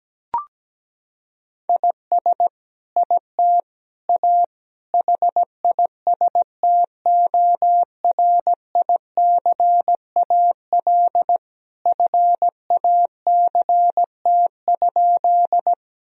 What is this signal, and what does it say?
Morse code, heard as a keyed tone, about 17 wpm, 700 Hz. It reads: IS IT A HISTORICAL FACT?
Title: Sentences from Top 1000 Words - Review Part 4 - 17wpm